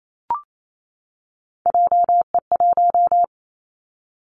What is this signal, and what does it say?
Morse code, heard as a keyed tone, about 28 wpm, 700 Hz. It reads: JE1
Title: Callsign Prefixes for Top 25 DXCC - 28wpm